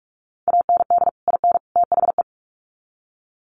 Morse code, heard as a keyed tone, about 44 wpm, 700 Hz. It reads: AND IN THE
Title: Sets of 3 Words 45wpm